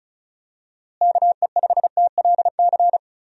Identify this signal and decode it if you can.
Morse code, heard as a keyed tone, about 34 wpm, 700 Hz. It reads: KE5TLC
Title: USA Call Signs - 6 Characters - Repeated 3x - 35wpm